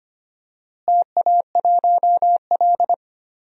Morse code, heard as a keyed tone, about 25 wpm, 700 Hz. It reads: TA1L
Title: DXCC-100 Call Signs - 4 Characters - Repeated 2x - 25wpm